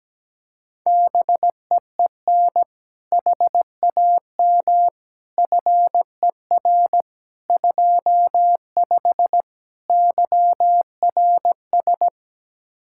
Morse code, heard as a keyed tone, about 17 wpm, 700 Hz. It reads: BEEN HAM FER 25 YRS
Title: Practice QSO - Exchange 3 - More Info - 17wpm